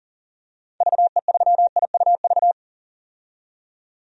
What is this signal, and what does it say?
Morse code, heard as a keyed tone, about 40 wpm, 700 Hz. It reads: VE3IUV
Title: DXCC-100 Call Signs - 6 Characters - 40wpm